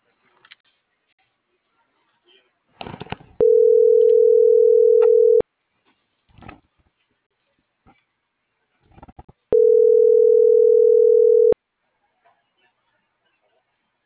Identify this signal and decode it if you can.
Ringback tone (North America), 440 + 480 Hz.